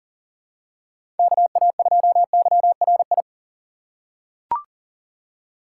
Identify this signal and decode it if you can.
Morse code, heard as a keyed tone, about 39 wpm, 700 Hz. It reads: KA2YRI
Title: USA Call Signs - 6 Characters - 2x Character Spacing - 40wpm